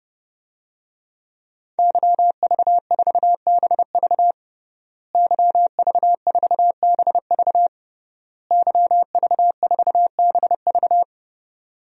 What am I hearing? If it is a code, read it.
Morse code, heard as a keyed tone, about 30 wpm, 700 Hz. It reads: YV4BV YV4BV YV4BV
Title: DXCC-100 Call Signs - 5 Characters - Repeated 3x - 30wpm